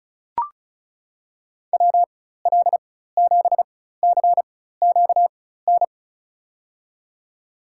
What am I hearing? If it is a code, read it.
Morse code, heard as a keyed tone, about 36 wpm, spaced as for 16 wpm, 700 Hz. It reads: WL7CQN
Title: USA Call Signs - 6 Characters - 4x Character Spacing - 35wpm